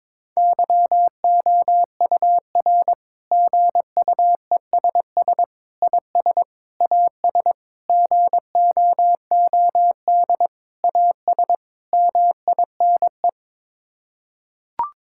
Morse code, heard as a keyed tone, about 22 wpm, 700 Hz. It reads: YOUR GUESS IS AS GOOD AS MINE
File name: practice / 22wpm / Sets of 7 Words - 22wpm